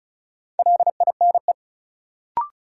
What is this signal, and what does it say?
Morse code, heard as a keyed tone, about 36 wpm, 700 Hz. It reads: LINE
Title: Sentences from Top 1200 Words - New Word Drill Alt - 35wpm